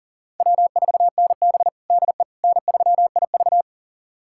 Morse code, heard as a keyed tone, about 40 wpm, 700 Hz. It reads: W4NB DE N3IV
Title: USA Callsign DE Callsign - 4 Characters - 40wpm